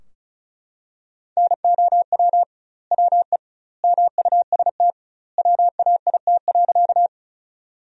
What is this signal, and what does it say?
Morse code, heard as a keyed tone, about 36 wpm, 700 Hz. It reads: NOW WE MUST WAIT.
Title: 2x Word-Spacing - Sentences from Top 500 Words - 35wpm